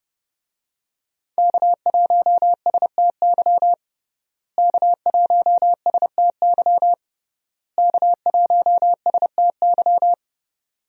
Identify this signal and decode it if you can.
Morse code, heard as a keyed tone, about 30 wpm, 700 Hz. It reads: K1STY K1STY K1STY
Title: USA Call Signs - 5 Characters - Repeated 3x - 30wpm